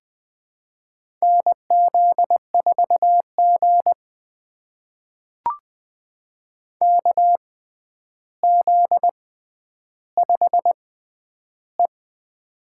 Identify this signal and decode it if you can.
Morse code, heard as a keyed tone, about 20 wpm, spaced as for 10 wpm, 700 Hz. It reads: NZ4G KZ5E
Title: USA Call Signs - 4 Characters - 6x Character Spacing - 20wpm